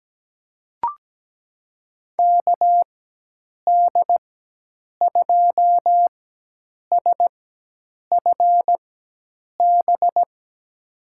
Morse code, heard as a keyed tone, about 17 wpm, spaced as for 8 wpm, 700 Hz. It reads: KD2SFB
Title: USA Call Signs - 6 Characters - 4x Character Spacing - 17wpm